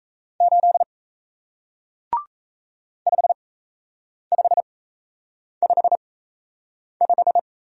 Morse code, heard as a keyed tone, about 38 wpm, spaced as for 8 wpm, 700 Hz. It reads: 8 5555